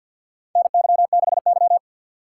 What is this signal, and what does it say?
Morse code, heard as a keyed tone, about 52 wpm, 700 Hz. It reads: NY6Y
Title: K1USN SST QSO Practice - 50wpm